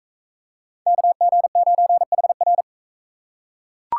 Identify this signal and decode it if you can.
Morse code, heard as a keyed tone, about 40 wpm, 700 Hz. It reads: KG9HR